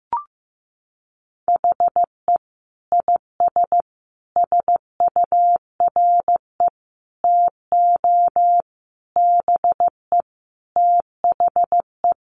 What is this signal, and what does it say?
Morse code, heard as a keyed tone, about 15 wpm, 700 Hz. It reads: HE IS SURE TO BE THE
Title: Sentences from Top 1400 Words 15wpm